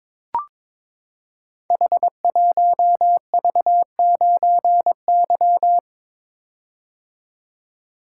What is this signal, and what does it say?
Morse code, heard as a keyed tone, about 23 wpm, 700 Hz. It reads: H1V9Y
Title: Code Groups - 5 Letter:Numbers - 22wpm